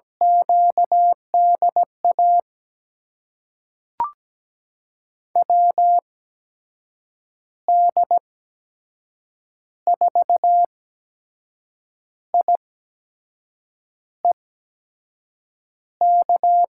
Morse code, heard as a keyed tone, about 17 wpm, spaced as for 6 wpm, 700 Hz. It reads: QDA WD4IEK
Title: USA Call Signs - 6 Characters - 8x Character Spacing - 17wpm